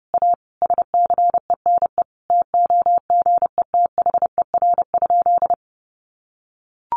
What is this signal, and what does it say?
Morse code, heard as a keyed tone, about 30 wpm, 700 Hz. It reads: A SCENE TOGETHER?
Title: Sentences from Top 1100 Words 30wpm